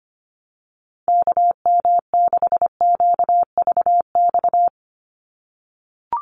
Morse code, heard as a keyed tone, about 25 wpm, 700 Hz. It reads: KM6QVX
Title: USA Call Signs 25wpm